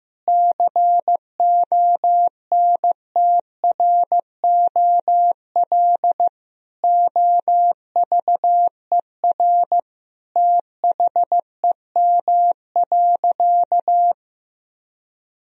Morse code, heard as a keyed tone, about 15 wpm, 700 Hz. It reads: CONTROL OVER THEM.
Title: Sentences from Top 1000 Words - Review Part 3 - 15wpm